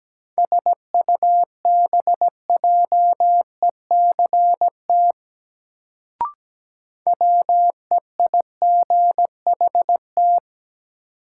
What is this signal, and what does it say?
Morse code, heard as a keyed tone, about 17 wpm, 700 Hz. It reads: SUBJECT WEIGHT
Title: Sentences from Top 800 Words - New Word Drill - 17wpm